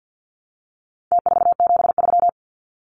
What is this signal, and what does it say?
Morse code, heard as a keyed tone, about 50 wpm, 700 Hz. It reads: T473
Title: Numbers - Contest Style - 4-Digit 50wpm